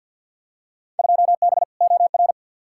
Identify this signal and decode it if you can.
Morse code, heard as a keyed tone, about 52 wpm, 700 Hz. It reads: 2B OR
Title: Field Day Exchanges Encore - 50wpm